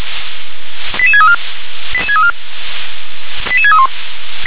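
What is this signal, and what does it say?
Five-tone selective call (ZVEI), heard as quick runs of five tones, tones of about 70 ms. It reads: 88634 88634 88631
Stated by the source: Five-tone ZVEI selective call received by SDR.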